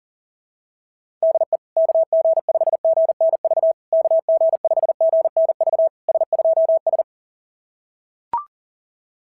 Morse code, heard as a keyed tone, about 38 wpm, 690 Hz. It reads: DE KG5GNV KG5GNV S2S
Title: SOTA QSO Practice 40wpm